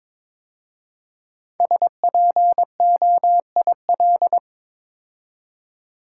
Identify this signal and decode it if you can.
Morse code, heard as a keyed tone, about 22 wpm, 700 Hz. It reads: SPOIL
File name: 5 Letter Words - First 3 Letters Given - 22wpm